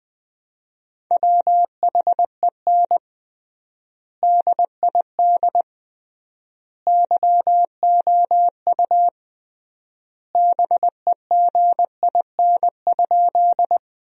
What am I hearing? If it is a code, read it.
Morse code, heard as a keyed tone, about 20 wpm, 700 Hz. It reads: WHEN DID YOU BEGIN?
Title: 3x Word-Spacing - Sentences from Top 300 Words - 20wpm